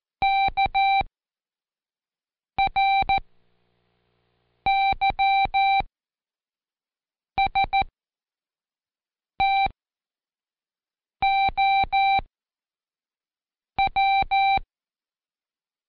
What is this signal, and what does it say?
Morse code, heard as a keyed tone, about 14 wpm, spaced as for 5 wpm, 780 Hz. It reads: KRYSTOW
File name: Code Quick / AudioDownld1 / Seg20-Day3-ReviewAll